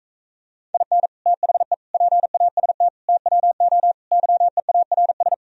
Morse code, heard as a keyed tone, about 45 wpm, 700 Hz. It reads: IN THE PAST TWO YEARS